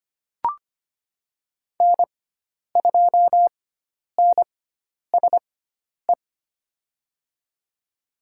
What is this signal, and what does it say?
Morse code, heard as a keyed tone, about 25 wpm, spaced as for 10 wpm, 700 Hz. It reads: N2NSE
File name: Lesson 16g - Letter c - Call Signs 25wpm